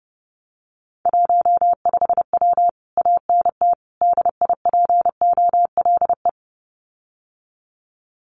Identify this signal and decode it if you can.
Morse code, heard as a keyed tone, about 30 wpm, 700 Hz. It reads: 15W ANT DIPOLE
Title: Common QSO Phrases Encore 30wpm